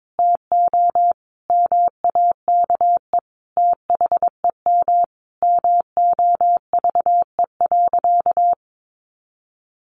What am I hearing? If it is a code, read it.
Morse code, heard as a keyed tone, about 22 wpm, 700 Hz. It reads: TO MAKE THEM MOVE.
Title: A Sound Of Thunder 22wpm